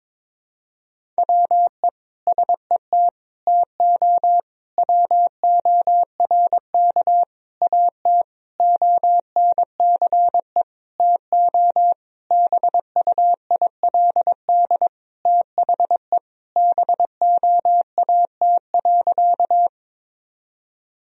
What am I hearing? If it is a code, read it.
Morse code, heard as a keyed tone, about 22 wpm, 700 Hz. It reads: WE SET TO WORK AT ONCE TO BUILD THE BOAT.
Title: Sentences from Top 1000 Words - Review Part 1 - 22wpm